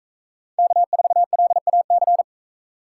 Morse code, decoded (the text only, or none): K4LAC